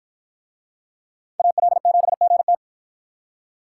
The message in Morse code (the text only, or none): AL7GT